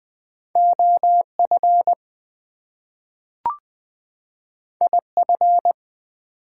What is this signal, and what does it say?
Morse code, heard as a keyed tone, about 20 wpm, 700 Hz. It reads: OF IF
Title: Two Letter Words - 20wpm